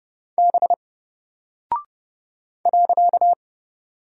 Morse code, heard as a keyed tone, about 30 wpm, 700 Hz. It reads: B .
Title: Single Character (Letter:Number:Punctuation:Prosign) 30wpm